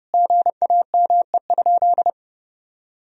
Morse code, heard as a keyed tone, about 30 wpm, 700 Hz. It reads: GAME?
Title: Sentences from Top 1000 Words - Review Part 4 - 30wpm